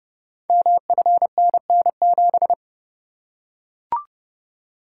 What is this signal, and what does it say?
Morse code, heard as a keyed tone, about 30 wpm, 700 Hz. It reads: MFNN7
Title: Code Groups - 5 Letter:Numbers - 30wpm